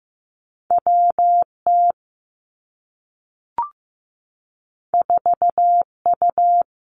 Morse code, heard as a keyed tone, about 15 wpm, 700 Hz. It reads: WT 4U
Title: Two Challenging Letter-Number Combinations 15wpm